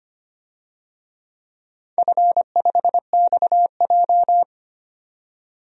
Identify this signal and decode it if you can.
Morse code, heard as a keyed tone, about 25 wpm, 690 Hz. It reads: F5XJ
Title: DXCC-100 Call Signs - 4 Characters - 25wpm